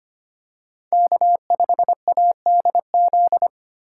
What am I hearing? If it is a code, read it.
Morse code, heard as a keyed tone, about 25 wpm, 700 Hz. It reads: K5ADZ
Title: USA Call Signs - 2x Character Spacing - 25wpm